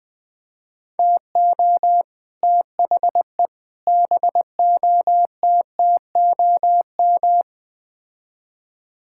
Morse code, heard as a keyed tone, about 20 wpm, 700 Hz. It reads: TO THE BOTTOM
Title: Build A Phrase - 5-Word Phrases - 20wpm